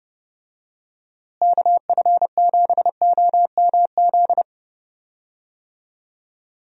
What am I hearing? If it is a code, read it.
Morse code, heard as a keyed tone, about 32 wpm, 700 Hz. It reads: KF7OMZ
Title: USA Call Signs - 6 Characters - Repeated 2x - 30wpm